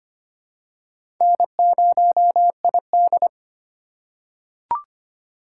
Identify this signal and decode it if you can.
Morse code, heard as a keyed tone, about 25 wpm, 700 Hz. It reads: N0ID
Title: USA Call Signs - 4 Characters - 3x Character Spacing - 25wpm